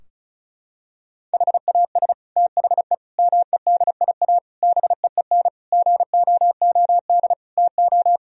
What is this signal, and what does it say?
Morse code, heard as a keyed tone, about 34 wpm, 700 Hz. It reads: HAS THE MEDIA BEEN GOOD TO
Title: Sentences from Top 1000 Words - Review Part 4 - 35wpm